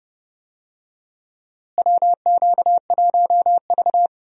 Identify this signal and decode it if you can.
Morse code, heard as a keyed tone, about 30 wpm, 700 Hz. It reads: WQ1V
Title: USA Call Signs - 4 Characters - Repeated 3x - 30wpm